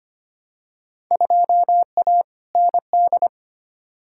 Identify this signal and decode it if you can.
Morse code, heard as a keyed tone, about 25 wpm, 700 Hz. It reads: 2A ND
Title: Field Day Exchanges - 25wpm